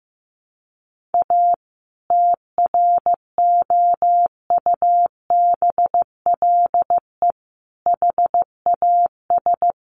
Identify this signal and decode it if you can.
Morse code, heard as a keyed tone, about 15 wpm, 700 Hz. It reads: A TROUBLE HAS
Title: Sentences from Top 1200 Words 15wpm